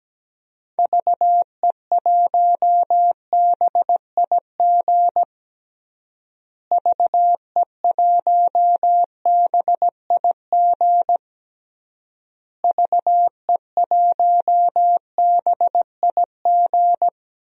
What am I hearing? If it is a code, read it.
Morse code, heard as a keyed tone, about 17 wpm, 700 Hz. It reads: VE1BIG VE1BIG VE1BIG
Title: DXCC-100 Call Signs - 6 Characters - Repeated 3x - 17wpm